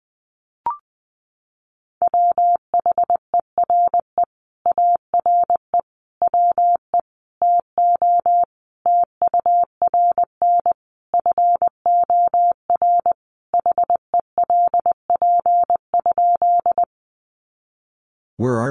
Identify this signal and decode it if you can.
Morse code, heard as a keyed tone, about 20 wpm, 700 Hz. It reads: WHERE ARE WE TO TURN FOR HELP?
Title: Sentences from Top 200 Words 20wpm